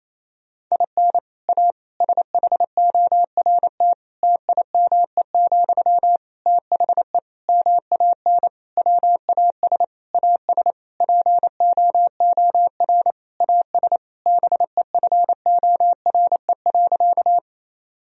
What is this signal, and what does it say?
Morse code, heard as a keyed tone, about 28 wpm, 700 Hz. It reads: IN A SHORT TIME, THE MAN WAS AS POOR AS BEFORE.
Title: Sentences from Top 1000 Words - Review Part 2 - 28wpm